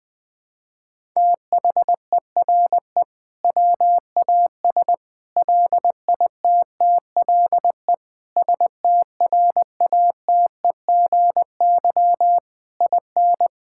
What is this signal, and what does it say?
Morse code, heard as a keyed tone, about 21 wpm, 700 Hz. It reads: THERE WAS LITTLE STRATEGY IN
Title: Sentences from Top 1000 Words - Review Part 5 - 20wpm